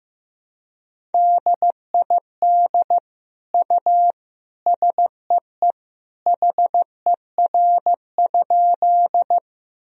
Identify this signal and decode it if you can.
Morse code, heard as a keyed tone, about 15 wpm, 700 Hz. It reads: DID U SEE HER?
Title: Sentences from Top 100 Words and Abbreviations 15wpm